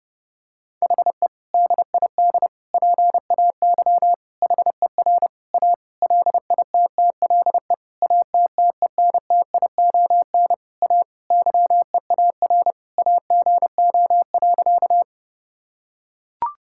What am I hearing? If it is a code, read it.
Morse code, heard as a keyed tone, about 32 wpm, 700 Hz. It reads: HE DID PAY HER A LITTLE ATTENTION A YEAR AGO.
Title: Sentences from Top 1000 Words - Review Part 6 - 30wpm